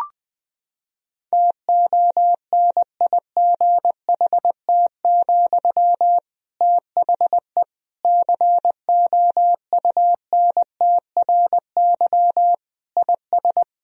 Morse code, heard as a keyed tone, about 21 wpm, 700 Hz. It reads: TONIGHT, THE COUNTRY IS
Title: Sentences from Top 1000 Words - Review Part 4 - 20wpm